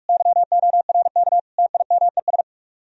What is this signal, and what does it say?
Morse code, heard as a keyed tone, about 47 wpm, 700 Hz. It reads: YORK TIMES